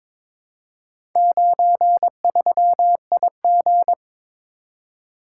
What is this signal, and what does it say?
Morse code, heard as a keyed tone, about 21 wpm, 700 Hz. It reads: 93IG
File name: practice / 22wpm / Four Letter-Number Combinations - 22wpm